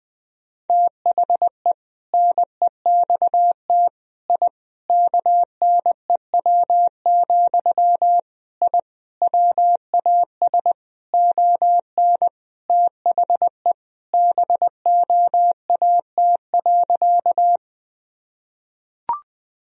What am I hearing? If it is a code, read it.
Morse code, heard as a keyed tone, about 20 wpm, 700 Hz. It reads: THE NEXT I KNEW, I WAS ON THE BOAT.
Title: Sentences from Top 1000 Words - Review Part 3 - 20wpm